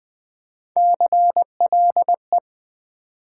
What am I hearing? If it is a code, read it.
Morse code, heard as a keyed tone, about 20 wpm, 700 Hz. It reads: CLE
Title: Build A Word - 3 Letters At A Time - 6-Letter Words - 20wpm